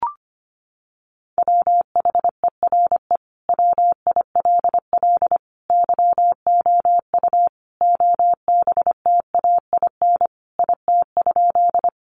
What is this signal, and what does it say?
Morse code, heard as a keyed tone, about 25 wpm, 700 Hz. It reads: WHERE WILL YOU OBTAIN IT?
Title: Sentences from Top 1300 Words 25wpm